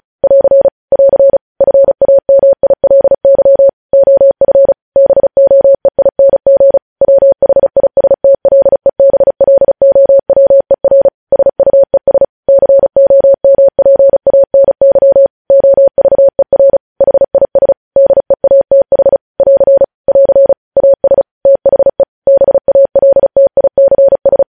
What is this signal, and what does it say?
Morse code, heard as a keyed tone, about 36 wpm, 550 Hz. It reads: <AR> <AR> FAMILY OF BOEING WHISTLEBLOWER SUES COMPANY OVER HIS DEATH <AR> <AR> AS THE BALTICS
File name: podcast / News headlines-1x-35x35